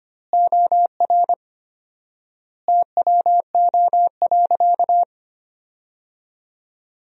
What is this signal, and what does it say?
Morse code, heard as a keyed tone, about 25 wpm, 700 Hz. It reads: OR TWO.
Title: 4x Word-Spacing - Sentences from Top 300 Words - 25wpm